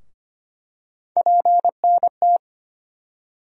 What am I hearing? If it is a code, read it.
Morse code, heard as a keyed tone, about 25 wpm, 700 Hz. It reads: PNT
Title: USA Call Sign Postfixes - 3 Letters 25wpm